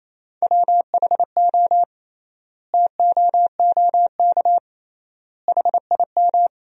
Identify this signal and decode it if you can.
Morse code, heard as a keyed tone, about 28 wpm, 700 Hz. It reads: WHO TOOK HIM
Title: 3x Word-Spacing - Sentences from Top 300 Words - 28wpm